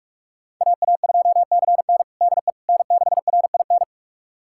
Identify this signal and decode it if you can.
Morse code, heard as a keyed tone, about 46 wpm, 700 Hz. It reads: AA2CN DE N6RIN
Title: USA Callsign DE Callsign - 5 Characters - 45wpm